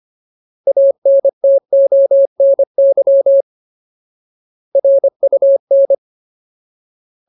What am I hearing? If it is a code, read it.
Morse code, heard as a keyed tone, about 25 wpm, 550 Hz. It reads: ANTONY RUN
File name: Ultimate Word List / 25WPM Ultimate